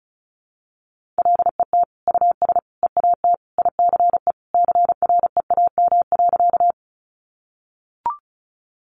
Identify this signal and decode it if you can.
Morse code, heard as a keyed tone, about 34 wpm, 700 Hz. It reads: LET US EAT ICE CREAM.
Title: Sentences from Top 1500 Words 35wpm